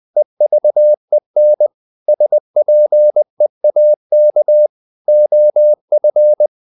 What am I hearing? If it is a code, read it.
Morse code, heard as a keyed tone, about 20 wpm, 600 Hz. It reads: EVEN SPEAK OF
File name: the old man and the sea part 2 20wpm